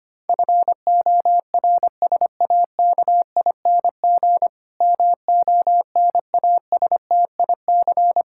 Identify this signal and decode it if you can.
Morse code, heard as a keyed tone, about 25 wpm, 700 Hz. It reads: FORSAKING MONASTIC
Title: Pangram Challenge v2 - 25wpm